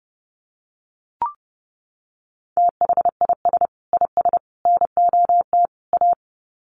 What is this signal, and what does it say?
Morse code, heard as a keyed tone, about 30 wpm, 700 Hz. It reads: THIS IS NOT A
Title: Sets of 4 Words 30wpm